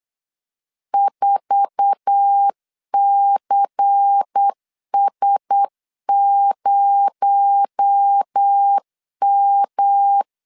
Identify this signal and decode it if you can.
Morse code, heard as a keyed tone, about 8 wpm, 800 Hz. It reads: 4CS0M
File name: LEZIONI / 8 YJ45 / 8 YJ4 8wpm 3246